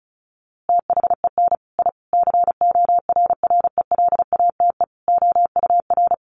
Morse code, heard as a keyed tone, about 34 wpm, 700 Hz. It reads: THEN I CORRELATE OUR
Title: A Sound Of Thunder 35wpm